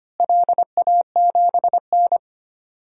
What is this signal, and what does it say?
Morse code, heard as a keyed tone, about 25 wpm, 700 Hz. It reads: LA7N
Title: DXCC-100 Call Signs - 4 Characters - Repeated 3x - 25wpm